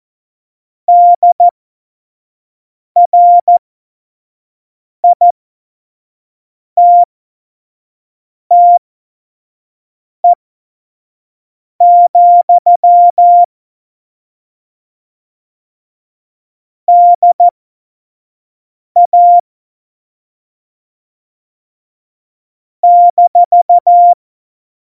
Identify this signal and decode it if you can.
Morse code, heard as a keyed tone, about 14 wpm, spaced as for 5 wpm, 700 Hz. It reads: DRITTE, DA -
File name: Morsekurs / Witze / Witz 13. November 2018, 25 F70